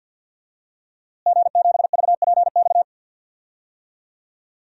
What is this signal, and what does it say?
Morse code, heard as a keyed tone, about 52 wpm, 700 Hz. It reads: G7VPX
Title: DXCC-100 Call Signs - 5 Characters - Repeated 3x - 50wpm